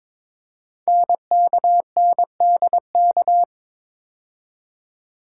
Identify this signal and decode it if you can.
Morse code, heard as a keyed tone, about 22 wpm, 700 Hz. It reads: NKNDK
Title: Code Groups - 5 Letters - 22wpm